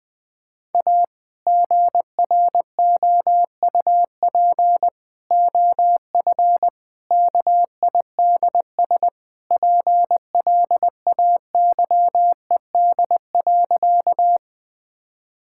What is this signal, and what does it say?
Morse code, heard as a keyed tone, about 20 wpm, 700 Hz. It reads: A GROUP OF KIDS PLAYED.
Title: Sentences with 3-Word Phrases - Spoken Beginning - 20wpm